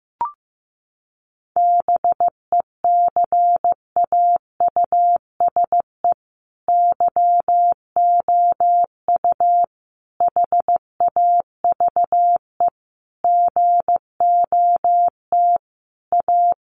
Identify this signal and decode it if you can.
Morse code, heard as a keyed tone, about 15 wpm, 700 Hz. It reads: BECAUSE YOU HAVE GOT A
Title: Sentences from Top 1500 Words 15wpm